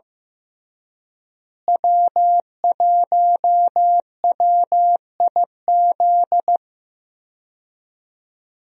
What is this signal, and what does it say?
Morse code, heard as a keyed tone, about 15 wpm, 700 Hz. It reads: W1WIZ
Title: USA Call Signs - 5 Characters - Repeated 3x - 15wpm